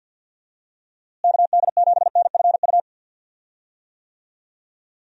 Morse code, heard as a keyed tone, about 50 wpm, 700 Hz. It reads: KD7NFU
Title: USA Call Signs - 6 Characters - Repeated 2x - 50wpm